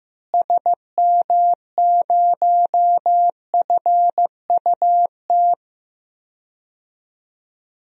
Morse code, heard as a keyed tone, about 15 wpm, 700 Hz. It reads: SM0FUT
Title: DXCC-100 Call Signs - 6 Characters - Repeated 2x - 15wpm